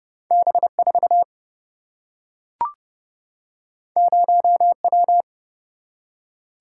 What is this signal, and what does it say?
Morse code, heard as a keyed tone, about 30 wpm, 700 Hz. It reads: B4 0W
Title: Single-Letter Single-Number 30wpm